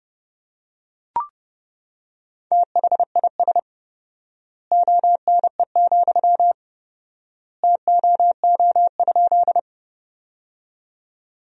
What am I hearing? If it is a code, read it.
Morse code, heard as a keyed tone, about 32 wpm, 700 Hz. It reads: THIS ONE, TOO?
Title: 4x Word-Spacing - Sentences from Top 200 Words - 30wpm